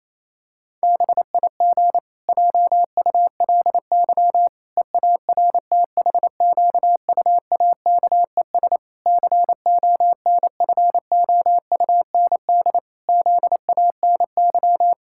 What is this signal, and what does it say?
Morse code, heard as a keyed tone, about 28 wpm, 700 Hz. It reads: BIG JULY EARTHQUAKES CONFOUND ZANY
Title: Pangram Challenge v2 - 28wpm